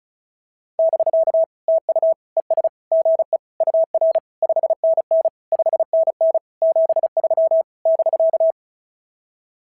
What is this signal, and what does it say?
Morse code, heard as a keyed tone, about 36 wpm, 650 Hz. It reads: <BK> TU ES GE UR 5NN 5NN 73 <BK>
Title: SOTA QSO Practice 35wpm